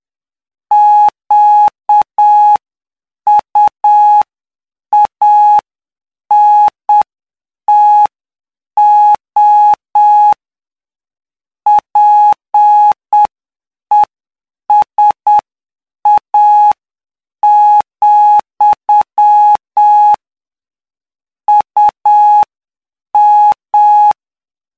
Morse code, heard as a keyed tone, about 7 wpm, spaced as for 6 wpm, 840 Hz. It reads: QUANTO PESA, UM
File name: cw / py2lcd / 16-PY2LCD-Li��o 25